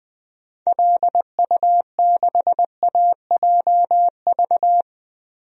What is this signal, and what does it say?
Morse code, heard as a keyed tone, about 20 wpm, 700 Hz. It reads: LU6AJV
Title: DXCC-100 Call Signs - 6 Characters - Repeated 2x - 20wpm